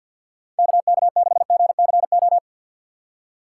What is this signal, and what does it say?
Morse code, heard as a keyed tone, about 51 wpm, 700 Hz. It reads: KK6GCO
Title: USA Call Signs - 6 Characters - 6x Character Spacing - 50wpm